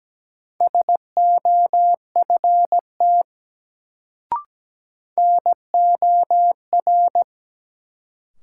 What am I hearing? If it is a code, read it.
Morse code, heard as a keyed tone, about 17 wpm, 700 Hz. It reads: SOFT NOR
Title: Sentences from Top 1500 Words - New Word Drill - 17wpm